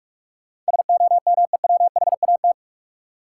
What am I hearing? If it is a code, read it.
Morse code, heard as a keyed tone, about 47 wpm, 700 Hz. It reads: SOMEWHAT
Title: Sentences from Top 1300 Words - New Word Drill Alt - 45wpm